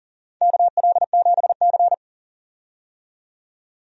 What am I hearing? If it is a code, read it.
Morse code, heard as a keyed tone, about 40 wpm, 700 Hz. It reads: KL7C
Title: DXCC-100 Call Signs - 4 Characters - 40wpm